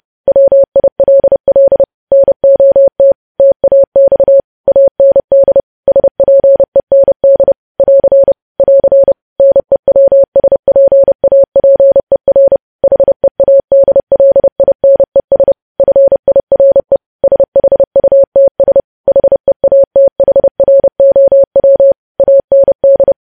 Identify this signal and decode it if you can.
Morse code, heard as a keyed tone, about 30 wpm, 550 Hz. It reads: WILL NOT TAX AND SPEND <AR> <AR> NEWSPAPER HEADLINES FIRE SHUTS HEATHROW AND
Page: news headlines in morse code at 30 wpm